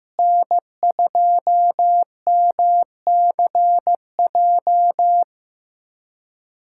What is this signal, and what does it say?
Morse code, heard as a keyed tone, about 15 wpm, 700 Hz. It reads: N2MCJ
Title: USA Callsign DE Callsign 15wpm